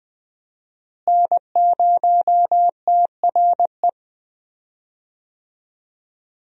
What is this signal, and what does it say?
Morse code, heard as a keyed tone, about 20 wpm, 700 Hz. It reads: N0TRE
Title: USA Call Signs - 5 Characters - Repeated 2x - 20wpm